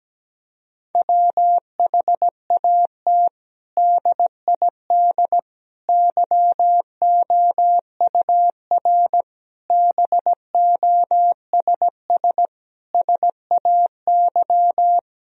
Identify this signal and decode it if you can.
Morse code, heard as a keyed tone, about 17 wpm, 700 Hz. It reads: WHAT DID YOUR BOSS SAY
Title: Sentences from Top 1000 Words - Review Part 4 - 17wpm